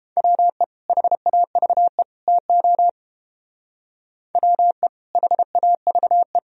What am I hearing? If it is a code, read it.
Morse code, heard as a keyed tone, about 28 wpm, 700 Hz. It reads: WE HAVE TO WE HAVE